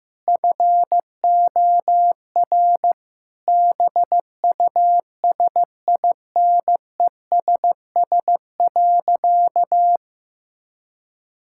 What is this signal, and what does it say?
Morse code, heard as a keyed tone, about 15 wpm, 700 Hz. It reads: FOR BUSINESS.
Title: Sentences from Top 1000 Words - Review Part 6 - 15wpm